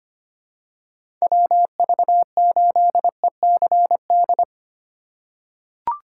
Morse code, heard as a keyed tone, about 25 wpm, 700 Hz. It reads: WV8ECD
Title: USA Call Signs - 2x Character Spacing - 25wpm